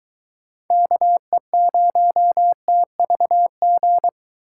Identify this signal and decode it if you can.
Morse code, heard as a keyed tone, about 22 wpm, 700 Hz. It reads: KE0TVG